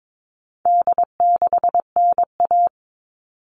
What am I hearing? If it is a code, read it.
Morse code, heard as a keyed tone, about 22 wpm, 700 Hz. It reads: D6NA
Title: All Cut Numbers - 4-Digit 22wpm